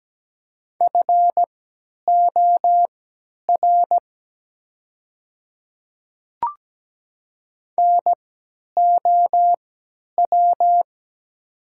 Morse code, heard as a keyed tone, about 17 wpm, spaced as for 10 wpm, 700 Hz. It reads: FOR NOW
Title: 3x Character Spacing - Top 100 Words - 17wpm